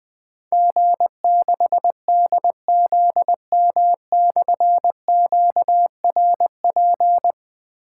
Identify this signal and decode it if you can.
Morse code, heard as a keyed tone, about 20 wpm, 700 Hz. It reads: G6DZM/QRP
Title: Difficult Call Signs 20wpm